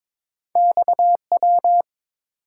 Morse code, heard as a keyed tone, about 30 wpm, spaced as for 26 wpm, 700 Hz. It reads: XW
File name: Speed-Racing - Two Letter-Number Combinations - 22wpm